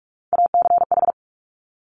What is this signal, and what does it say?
Morse code, heard as a keyed tone, about 46 wpm, 700 Hz. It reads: ACH
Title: Top Three Letter Combinations in Words 45wpm